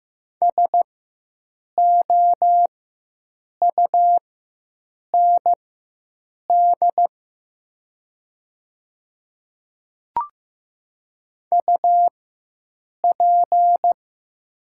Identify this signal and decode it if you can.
Morse code, heard as a keyed tone, about 15 wpm, spaced as for 7 wpm, 700 Hz. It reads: SOUND UP
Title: 4x Character Spacing - Top 100 Words - 15wpm